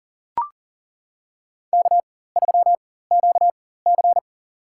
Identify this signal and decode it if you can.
Morse code, heard as a keyed tone, about 40 wpm, spaced as for 19 wpm, 700 Hz. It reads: K3QC
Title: USA Call Signs - 4 Characters - 4x Character Spacing - 40wpm